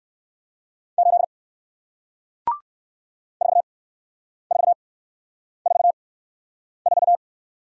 Morse code, heard as a keyed tone, about 58 wpm, spaced as for 9 wpm, 700 Hz. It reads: 8 4444